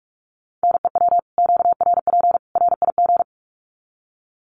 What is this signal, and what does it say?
Morse code, heard as a keyed tone, about 44 wpm, 700 Hz. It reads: NEW QRP RIG
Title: Common QSO Phrases 45wpm